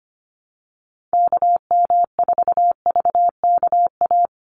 Morse code, heard as a keyed tone, about 25 wpm, 700 Hz. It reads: KM4VKA
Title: USA Call Signs 25wpm